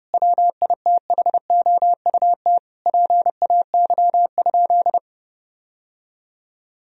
Morse code, heard as a keyed tone, about 30 wpm, 700 Hz. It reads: WITHOUT PAY?
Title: Sentences from Top 1000 Words - Review Part 3 - 30wpm